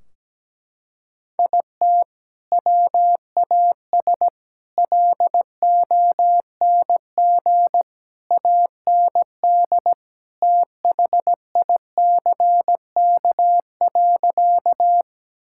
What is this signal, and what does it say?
Morse code, heard as a keyed tone, about 17 wpm, 700 Hz. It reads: IT WAS LONG AND THICK.
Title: Sentences from Top 1600 Words 17wpm